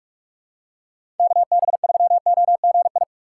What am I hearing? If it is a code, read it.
Morse code, heard as a keyed tone, about 46 wpm, 700 Hz. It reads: KB3OGI
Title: USA Call Signs - 4x Character Spacing - 45wpm